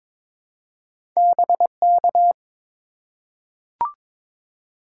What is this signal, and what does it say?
Morse code, heard as a keyed tone, about 21 wpm, 700 Hz. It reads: BK
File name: Top 90 QSO Elements - 2x Character Spacing - 22wpm